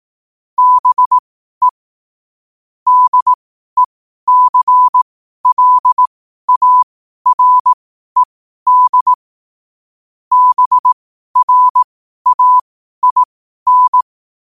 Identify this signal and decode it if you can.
Morse code, heard as a keyed tone, about 18 wpm, spaced as for 12 wpm, 1000 Hz. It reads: BE DECLARED BRAIN
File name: qotdfiles / 12wpm